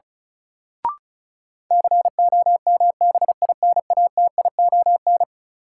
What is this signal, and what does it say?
Morse code, heard as a keyed tone, about 36 wpm, 700 Hz. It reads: COMBINATION
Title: Sentences from Top 1000 Words - Review Part 6 - New Word Drill - 35wpm